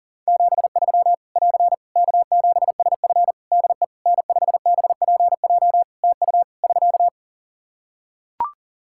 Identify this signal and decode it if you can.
Morse code, heard as a keyed tone, about 40 wpm, 700 Hz. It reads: 73 <AR> K7SF DE N5BPJ TU <SK>
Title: Practice QSO - Exchange 4 - Ending - 40wpm